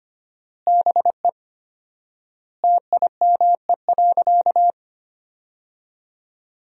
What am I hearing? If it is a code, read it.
Morse code, heard as a keyed tone, about 25 wpm, 700 Hz. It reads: BE TIME.
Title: 4x Word-Spacing - Sentences from Top 200 Words - 25wpm